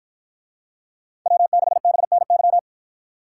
Morse code, heard as a keyed tone, about 53 wpm, 700 Hz. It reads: W6BNY